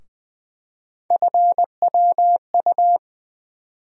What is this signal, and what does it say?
Morse code, heard as a keyed tone, about 20 wpm, 700 Hz. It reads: FWU
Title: USA Call Sign Postfixes - 3 Letters 20wpm